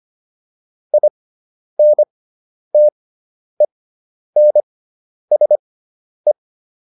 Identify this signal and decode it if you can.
Morse code, heard as a keyed tone, about 25 wpm, spaced as for 10 wpm, 600 Hz. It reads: INTENSE